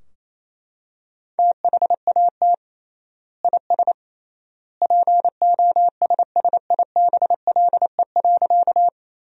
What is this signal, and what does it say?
Morse code, heard as a keyed tone, about 27 wpm, 700 Hz. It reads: THAT IS POSSIBLE.
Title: 3x Word-Spacing - Sentences from Top 500 Words - 28wpm